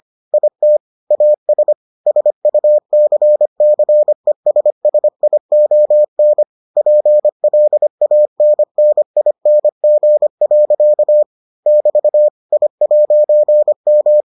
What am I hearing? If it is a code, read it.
Morse code, heard as a keyed tone, about 25 wpm, 600 Hz. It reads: IT AS SUCCESSION PLANNING. <BT> I'M